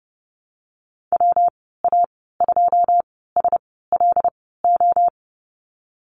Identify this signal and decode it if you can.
Morse code, heard as a keyed tone, about 30 wpm, spaced as for 17 wpm, 700 Hz. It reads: WA2SLO
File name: Lesson 19g - Letter w - Call Signs 30wpm